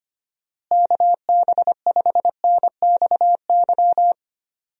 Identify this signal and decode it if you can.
Morse code, heard as a keyed tone, about 25 wpm, 700 Hz. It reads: KB5NXY
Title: USA Call Signs - 6 Characters - Repeated 3x - 25wpm